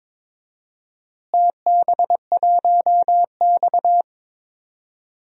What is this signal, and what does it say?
Morse code, heard as a keyed tone, about 22 wpm, 700 Hz. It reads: TB1X
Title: DXCC-100 Call Signs - 4 Characters - Repeated 2x - 22wpm